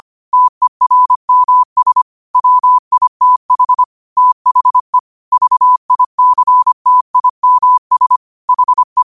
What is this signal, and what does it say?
Morse code, heard as a keyed tone, about 25 wpm, 1000 Hz. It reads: TERMS WITH THE VICTIMS HE
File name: qotdfiles / 25wpm